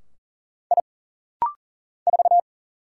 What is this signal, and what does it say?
Morse code, heard as a keyed tone, about 39 wpm, 700 Hz. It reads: I 4
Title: Single Letter-Number - Rapid-Fire 40wpm